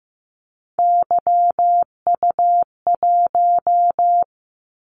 Morse code, heard as a keyed tone, about 15 wpm, 700 Hz. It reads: YU1
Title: Callsign Prefixes for Top 25 DXCC - 15wpm